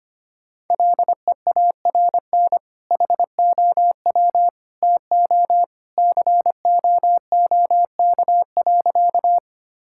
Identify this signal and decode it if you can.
Morse code, heard as a keyed tone, about 25 wpm, 700 Hz. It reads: LEARN HOW TO COOK.
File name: practice / 25wpm / Sentences with 2-Word Phrases - 25wpm